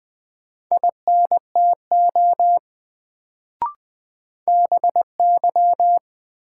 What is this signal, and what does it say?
Morse code, heard as a keyed tone, about 20 wpm, 700 Hz. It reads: INTO BY
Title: Sets of 2 Words - New Word Drill - 20wpm